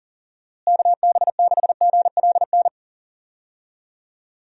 Morse code, heard as a keyed tone, about 40 wpm, 700 Hz. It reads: KB6GLN
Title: USA Call Signs - 6 Characters - Repeated 3x - 40wpm